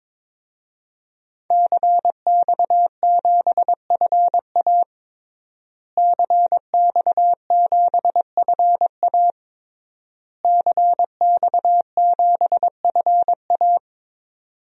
Morse code, heard as a keyed tone, about 22 wpm, 700 Hz. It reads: CX7FA CX7FA CX7FA
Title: DXCC-100 Call Signs - 5 Characters - Repeated 3x - 22wpm